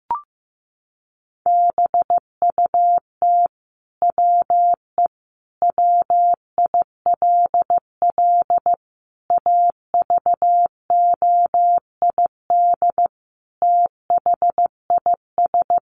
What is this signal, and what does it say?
Morse code, heard as a keyed tone, about 15 wpm, 700 Hz. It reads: BUT WE WILL AVOID THIS
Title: Sentences from Top 1200 Words 15wpm